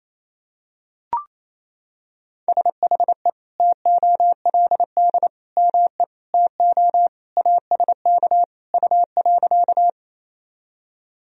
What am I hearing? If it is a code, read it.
Morse code, heard as a keyed tone, about 28 wpm, 700 Hz. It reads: SHE TOLD ME TO ASK U.
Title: Sentences from Top 400 Words and Abbreviations 28wpm